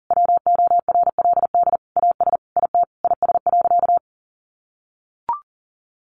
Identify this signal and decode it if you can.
Morse code, heard as a keyed tone, about 40 wpm, 700 Hz. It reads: WORLD AS IT IS.
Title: Sentences from Top 1200 Words 40wpm